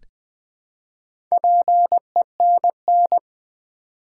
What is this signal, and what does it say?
Morse code, heard as a keyed tone, about 20 wpm, 700 Hz. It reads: PENN
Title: Build A Word - 1 Letter At A Time - 3 to 5-Letter Words - 20wpm